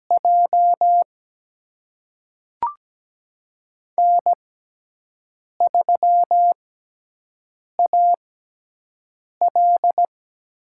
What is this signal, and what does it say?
Morse code, heard as a keyed tone, about 17 wpm, spaced as for 6 wpm, 700 Hz. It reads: J N3AL
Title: USA Call Signs - 6x Character Spacing - 17wpm